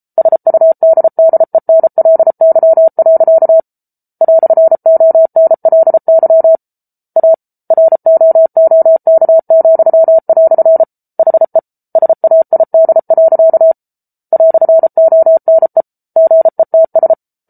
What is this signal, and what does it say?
Morse code, heard as a keyed tone, about 34 wpm, 650 Hz. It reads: SUDDENLY. "ONLY A ROOK," HE SAID. "ONE GETS